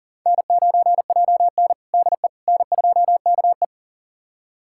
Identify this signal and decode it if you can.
Morse code, heard as a keyed tone, about 40 wpm, 700 Hz. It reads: N9JN DE N2KE
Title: USA Callsign DE Callsign - 4 Characters - 40wpm